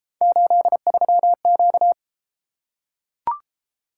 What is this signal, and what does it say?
Morse code, heard as a keyed tone, about 32 wpm, 700 Hz. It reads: 83Q